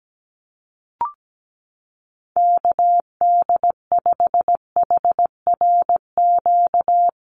Morse code, heard as a keyed tone, about 17 wpm, 700 Hz. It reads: KD5HRQ
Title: USA Call Signs Repeated 3x 17wpm